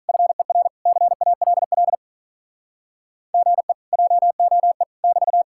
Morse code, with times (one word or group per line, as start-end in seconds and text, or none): 0.1-0.7 FER
0.9-1.9 CALL
3.3-3.7 GE
3.9-4.8 JOE
5.0-5.4 <BT>